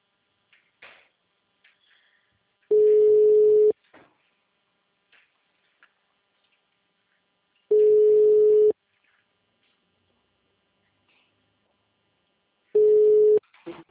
Ringback tone (Europe, 425 Hz).
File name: out- -2005- - - .11914